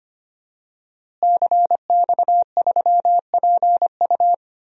Morse code, heard as a keyed tone, about 25 wpm, 700 Hz. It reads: CX3PU
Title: DXCC-100 Call Signs - 5 Characters - Repeated 2x - 25wpm